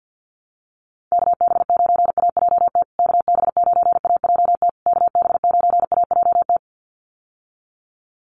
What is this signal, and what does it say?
Morse code, heard as a keyed tone, about 50 wpm, 700 Hz. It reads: KB9AJT KB9AJT KB9AJT
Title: USA Call Signs Repeated 3x 50wpm